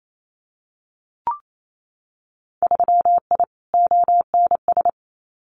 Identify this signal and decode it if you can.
Morse code, heard as a keyed tone, about 28 wpm, 700 Hz. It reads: 3I ONS
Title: Winter Field Day Exchanges Encore 28wpm